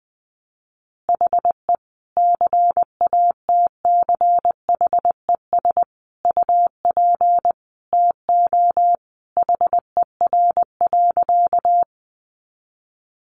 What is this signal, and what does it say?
Morse code, heard as a keyed tone, about 20 wpm, 700 Hz. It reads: HE CATCHES UP TO HER.
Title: Additional Easy Phrases 20wpm